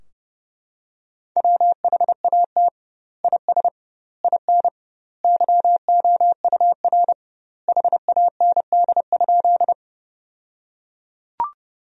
Morse code, heard as a keyed tone, about 32 wpm, 700 Hz. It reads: WHAT IS IN YOUR HAND?
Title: 2x Word-Spacing - Sentences from Top 200 Words - 30wpm